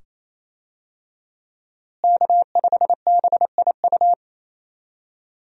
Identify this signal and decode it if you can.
Morse code, heard as a keyed tone, about 28 wpm, 700 Hz. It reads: K5BIU
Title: USA Call Signs - 5 Characters - Repeated 2x - 28wpm